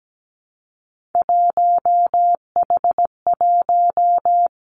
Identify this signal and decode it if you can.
Morse code, heard as a keyed tone, about 17 wpm, 700 Hz. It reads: 1H1
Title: Three Challenging Letter-Number Combinations 17wpm